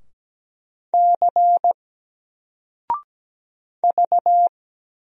Morse code, heard as a keyed tone, about 17 wpm, 700 Hz. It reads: C V
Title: Single Character (Letter:Number:Punctuation:Prosign) 17wpm